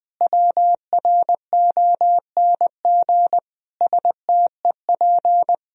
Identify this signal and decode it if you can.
Morse code, heard as a keyed tone, about 20 wpm, 700 Hz. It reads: WRONG STEP
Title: Sentences from Top 1000 Words - Review Part 2 - 20wpm